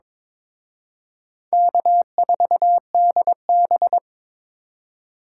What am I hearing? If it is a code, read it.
Morse code, heard as a keyed tone, about 26 wpm, 700 Hz. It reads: K4DB
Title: Speed Racing - USA 1x2 Call Signs 22wpm